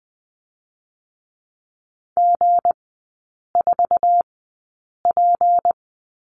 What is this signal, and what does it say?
Morse code, heard as a keyed tone, about 20 wpm, spaced as for 8 wpm, 700 Hz. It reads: G4P
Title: 218 Lesson 35e - Number 0 - Three Characters 20wpm